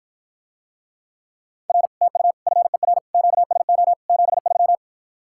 Morse code, heard as a keyed tone, about 54 wpm, 700 Hz. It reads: R TU FER QSO 73